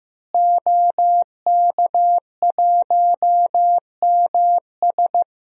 Morse code, heard as a keyed tone, about 15 wpm, 700 Hz. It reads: OK1MS
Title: DXCC-100 Call Signs - 5 Characters - Repeated 2x - 15wpm